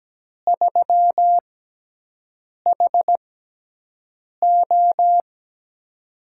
Morse code, heard as a keyed tone, about 17 wpm, spaced as for 6 wpm, 700 Hz. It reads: 3HO
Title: USA Call Signs - 4 Characters - 6x Character Spacing - 17wpm